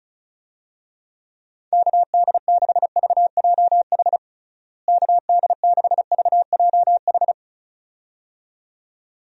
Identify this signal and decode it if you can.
Morse code, heard as a keyed tone, about 34 wpm, 700 Hz. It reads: KD6VJH KD6VJH
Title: USA Call Signs - 6 Characters - Repeated 2x - 35wpm